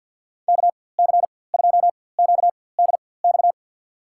Morse code, heard as a keyed tone, about 48 wpm, spaced as for 23 wpm, 700 Hz. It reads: KC2QDX
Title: USA Call Signs - 6 Characters - 4x Character Spacing - 50wpm